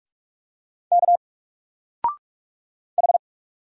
Morse code, heard as a keyed tone, about 46 wpm, 700 Hz. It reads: K H
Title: Single Letter-Number 45wpm